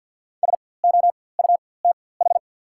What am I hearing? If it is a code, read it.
Morse code, heard as a keyed tone, about 48 wpm, spaced as for 23 wpm, 690 Hz. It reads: SOUTH
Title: 4x Character Spacing - Top 500 Words - 50wpm